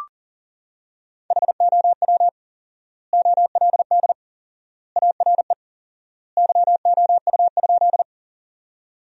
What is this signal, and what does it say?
Morse code, heard as a keyed tone, about 40 wpm, 700 Hz. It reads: HOW OLD ARE YOU?
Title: 4x Word-Spacing - Sentences from Top 200 Words - 40wpm